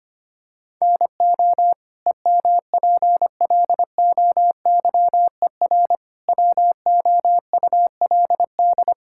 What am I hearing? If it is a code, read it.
Morse code, heard as a keyed tone, about 25 wpm, 700 Hz. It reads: NO EMPLOYER WOULD
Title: Sentences from Top 1600 Words 25wpm